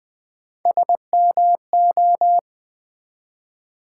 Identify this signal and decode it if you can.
Morse code, heard as a keyed tone, about 20 wpm, 700 Hz. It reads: SMO
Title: Build A Word - 1 Letter At A Time - 3 to 5-Letter Words - 20wpm